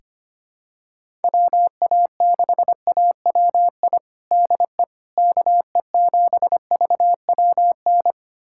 Morse code, heard as a keyed tone, about 25 wpm, 700 Hz. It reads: WA6AWI DE KE7VWN
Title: USA Callsign DE Callsign - 6 Characters - 25wpm